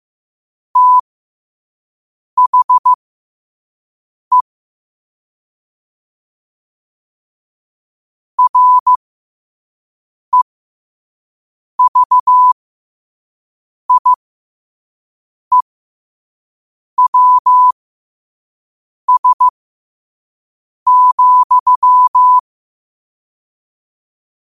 Morse code, heard as a keyed tone, about 15 wpm, spaced as for 5 wpm, 1000 Hz. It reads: THE REVIEWS,
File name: qotdfiles / 5wpm